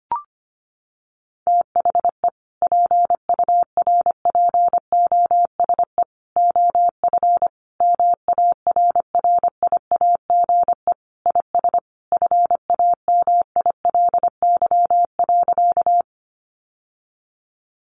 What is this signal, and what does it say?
Morse code, heard as a keyed tone, about 25 wpm, 700 Hz. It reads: THE PURPOSE OF MARRIAGE IS FAMILY.
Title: Sentences from Top 800 Words 25wpm